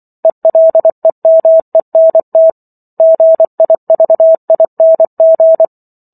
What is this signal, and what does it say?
Morse code, heard as a keyed tone, about 24 wpm, 650 Hz. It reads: ELEMENT GIVING